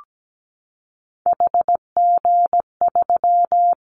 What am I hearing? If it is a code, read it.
Morse code, heard as a keyed tone, about 17 wpm, 700 Hz. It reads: HG3
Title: Callsign Prefixes for Top 50 DXCC - 17wpm